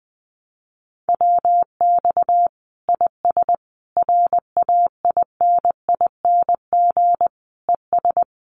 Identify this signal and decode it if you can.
Morse code, heard as a keyed tone, about 20 wpm, 700 Hz. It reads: WX IS RAINING ES
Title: Common QSO Phrases 20wpm